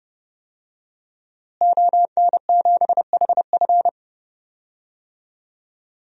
Morse code, heard as a keyed tone, about 30 wpm, 700 Hz. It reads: ON7HF
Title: DXCC-100 Call Signs - 5 Characters - Repeated 3x - 30wpm